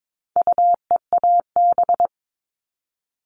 Morse code, heard as a keyed tone, about 22 wpm, 700 Hz. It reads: UEAB
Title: All Cut Numbers - 4-Digit 22wpm